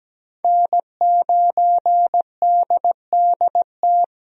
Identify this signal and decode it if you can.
Morse code, heard as a keyed tone, about 17 wpm, 700 Hz. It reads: N9DDT
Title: USA Call Signs - 5 Characters - Repeated 3x - 17wpm